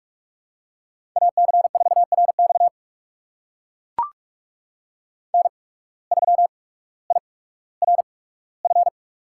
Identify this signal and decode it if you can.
Morse code, heard as a keyed tone, about 44 wpm, spaced as for 19 wpm, 700 Hz. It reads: AC4RX N3IRF
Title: USA Call Signs - 8x Character Spacing - 45wpm